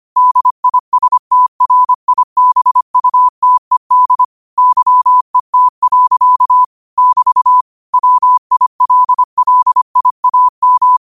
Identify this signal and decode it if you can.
Morse code, heard as a keyed tone, about 25 wpm, 1000 Hz. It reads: DISTRIBUTED YET. <BT> WILLIAM
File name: qotdfiles / 25wpm